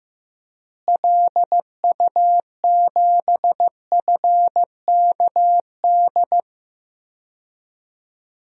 Morse code, heard as a keyed tone, about 15 wpm, 700 Hz. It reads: LU7FKD
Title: DXCC-100 Call Signs - 6 Characters - Repeated 2x - 15wpm